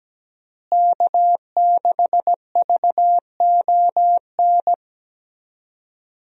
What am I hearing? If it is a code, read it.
Morse code, heard as a keyed tone, about 17 wpm, 700 Hz. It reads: K6VON
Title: USA Call Signs - 5 Characters - Repeated 3x - 17wpm